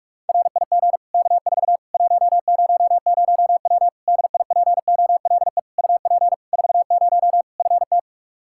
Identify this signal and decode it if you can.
Morse code, heard as a keyed tone, about 46 wpm, 700 Hz. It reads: RIG K4 100W DIPOLE UP 40 FT
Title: Practice QSO - Exchange 2 - Info - 45wpm